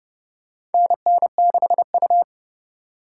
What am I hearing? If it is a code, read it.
Morse code, heard as a keyed tone, about 30 wpm, 700 Hz. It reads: NN6U
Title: USA Call Signs - 4 Characters - 2x Character Spacing - 30wpm